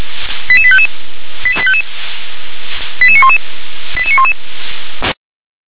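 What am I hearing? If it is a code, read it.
Five-tone selective call (ZVEI), heard as quick runs of five tones, tones of about 70 ms. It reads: 88855 88855 88911 88911
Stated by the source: ZVEI five-tone call received on SDR.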